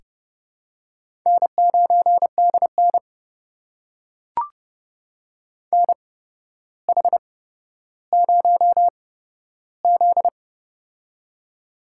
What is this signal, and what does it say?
Morse code, heard as a keyed tone, about 30 wpm, spaced as for 13 wpm, 700 Hz. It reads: N9DN NH0Z
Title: USA Call Signs - 4 Characters - 8x Character Spacing - 30wpm